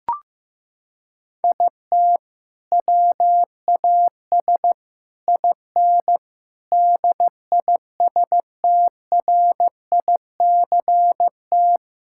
Morse code, heard as a keyed tone, about 15 wpm, 700 Hz. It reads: IT WAS IN DISTRICT
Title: Sentences from Top 1600 Words 15wpm